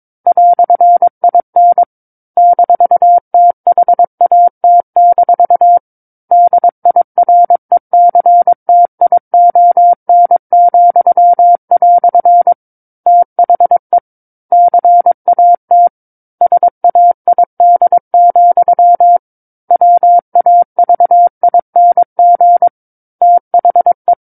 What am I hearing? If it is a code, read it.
Morse code, heard as a keyed tone, about 22 wpm, 700 Hz. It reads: "IN -THAT- DIRECTION," THE CAT SAID, WAVING THE
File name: 06-Chapter-06 22 WPM